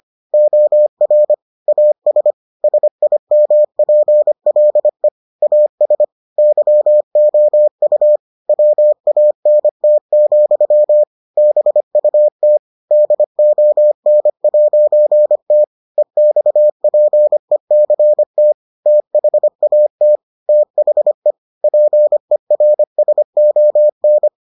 Morse code, heard as a keyed tone, about 25 wpm, 600 Hz. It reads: OR AS SIMPLE AS YOU WANT, BUT DON'T EXPECT THAT THE PERSON